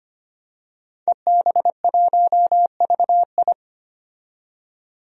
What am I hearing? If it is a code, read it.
Morse code, heard as a keyed tone, about 25 wpm, 700 Hz. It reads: EB1VI
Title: DXCC-100 Call Signs - 5 Characters - 25wpm